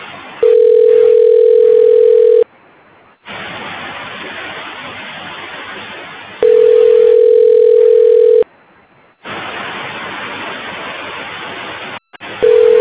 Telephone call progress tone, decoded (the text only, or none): ringback tone